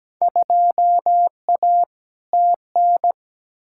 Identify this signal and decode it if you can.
Morse code, heard as a keyed tone, about 17 wpm, 700 Hz. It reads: 2A TN
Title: Field Day Exchanges - 17wpm